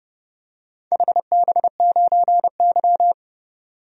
Morse code, heard as a keyed tone, about 30 wpm, 700 Hz. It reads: HB9Y
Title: DXCC-100 Call Signs - 4 Characters - 30wpm